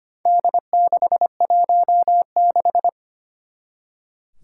Morse code, heard as a keyed tone, about 25 wpm, 700 Hz. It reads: D616
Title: Four Challenging Letter-Number Combinations 25wpm